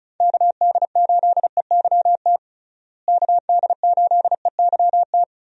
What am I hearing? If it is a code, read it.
Morse code, heard as a keyed tone, about 36 wpm, 700 Hz. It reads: KD8EYT KD8EYT
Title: USA Call Signs - 6 Characters - Repeated 3x - 35wpm